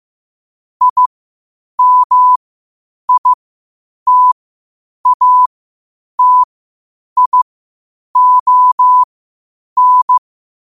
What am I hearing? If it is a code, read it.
Morse code, heard as a keyed tone, about 15 wpm, spaced as for 8 wpm, 1000 Hz. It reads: IMITATION
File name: qotdfiles / 8wpm